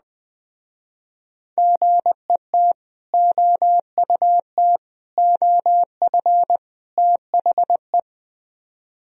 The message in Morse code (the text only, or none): GET OUT OF THE